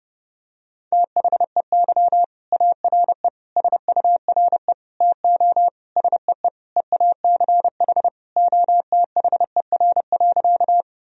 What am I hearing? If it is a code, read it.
Morse code, heard as a keyed tone, about 30 wpm, 700 Hz. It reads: THEY ARE SURE TO SEE EACH OTHER.
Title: Sentences from Top 1000 Words - Review Part 1 - 30wpm